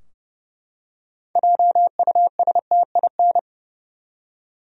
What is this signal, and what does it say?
Morse code, heard as a keyed tone, about 30 wpm, 700 Hz. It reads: JUSTIN
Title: USA Common Names - 30wpm